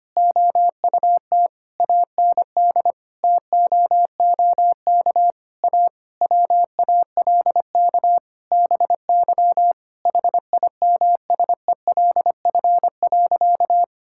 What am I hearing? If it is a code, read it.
Morse code, heard as a keyed tone, about 25 wpm, 700 Hz. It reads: OUT AND TOOK A WALK BY HIMSELF.
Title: Sentences from Top 1600 Words 25wpm